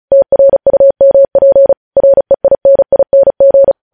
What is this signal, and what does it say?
Morse code, heard as a keyed tone, about 36 wpm, 550 Hz. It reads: TRUMP REINING
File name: podcast / News headlines-1x-35x35